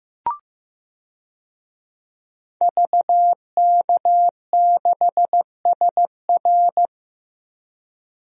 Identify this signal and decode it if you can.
Morse code, heard as a keyed tone, about 15 wpm, 700 Hz. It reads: VK6SR
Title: DXCC-100 Call Signs - 5 Characters - 15wpm